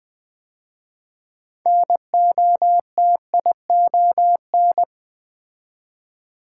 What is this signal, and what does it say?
Morse code, heard as a keyed tone, about 20 wpm, 700 Hz. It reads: NOTION
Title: 6 Letter Words - First 3 Letters Given - 20wpm